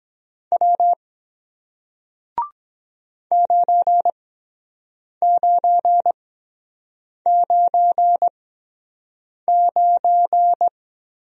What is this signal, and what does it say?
Morse code, heard as a keyed tone, about 23 wpm, spaced as for 7 wpm, 700 Hz. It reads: W 9999